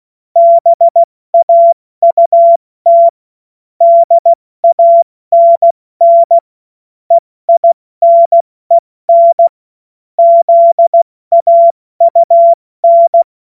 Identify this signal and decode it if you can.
Morse code, heard as a keyed tone, about 16 wpm, 680 Hz. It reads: BAUT DANN EINEN ZAUN
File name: Morsekurs / Witze / Witz 26 Dezember 2023 70 F80